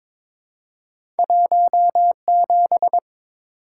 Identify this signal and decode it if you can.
Morse code, heard as a keyed tone, about 22 wpm, 700 Hz. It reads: 17
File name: Two Letter-Number Combinations - 22wpm